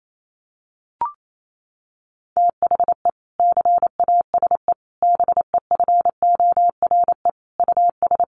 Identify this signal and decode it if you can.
Morse code, heard as a keyed tone, about 28 wpm, 700 Hz. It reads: THE CASE BEFORE US
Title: Sentences from Top 1400 Words 28wpm